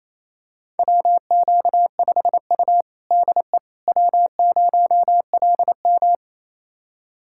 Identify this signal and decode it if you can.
Morse code, heard as a keyed tone, about 28 wpm, 700 Hz. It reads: WQ5U DE W0LM
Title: USA Callsign DE Callsign - 4 Characters - 28wpm